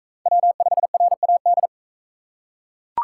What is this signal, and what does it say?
Morse code, heard as a keyed tone, about 39 wpm, 700 Hz. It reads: W5RAD